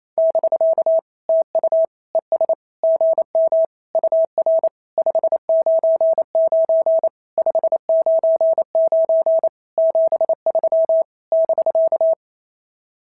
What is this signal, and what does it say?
Morse code, heard as a keyed tone, about 29 wpm, 690 Hz. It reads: <BK> TU ES GM UR 599 599 73 <BK>
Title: SOTA QSO Practice 28wpm